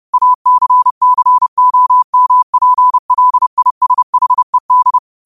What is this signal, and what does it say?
Morse code, heard as a keyed tone, about 30 wpm, 1000 Hz. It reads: ACCOMPLISHED
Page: Quotes for Fri, 15 Aug 2025 in Morse Code at 30 words per minute.